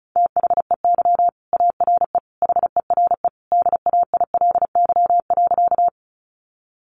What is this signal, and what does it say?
Morse code, heard as a keyed tone, about 34 wpm, 700 Hz. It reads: THEY ARE HERE DAILY.
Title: Sentences from Top 1100 Words 35wpm